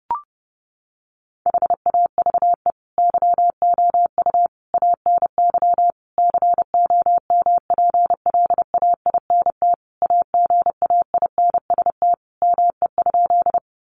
Morse code, heard as a keyed tone, about 30 wpm, 700 Hz. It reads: HAVE YOU ANY COMPLAINT AGAINST ME?
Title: Sentences from Top 1400 Words 30wpm